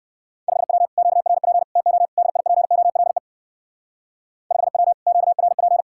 Morse code, heard as a keyed tone, about 59 wpm, 700 Hz. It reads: 5W QRP TO DIPOLE 5W QRP